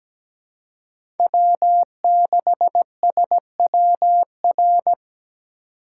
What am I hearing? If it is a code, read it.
Morse code, heard as a keyed tone, about 17 wpm, 700 Hz. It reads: W6SWR
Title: USA Call Signs - 5 Characters - 2x Character Spacing - 17wpm